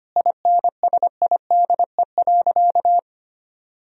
Morse code, heard as a keyed tone, about 25 wpm, 700 Hz. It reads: INSIDE.
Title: Sentences from Top 1000 Words - Review Part 2 - 25wpm